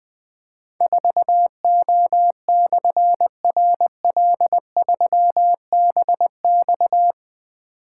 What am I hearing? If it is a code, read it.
Morse code, heard as a keyed tone, about 20 wpm, 700 Hz. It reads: 4O/RL3BX
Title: Difficult Call Signs 20wpm